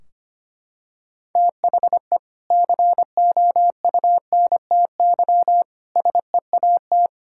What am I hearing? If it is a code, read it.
Morse code, heard as a keyed tone, about 25 wpm, 700 Hz. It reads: THE COUNTY SEAT
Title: Sentences from Top 1600 Words 25wpm